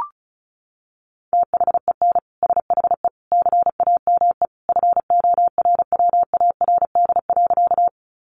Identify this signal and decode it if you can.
Morse code, heard as a keyed tone, about 34 wpm, 700 Hz. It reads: THEN SHE CAME FORWARD.
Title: Sentences from Top 1300 Words 35wpm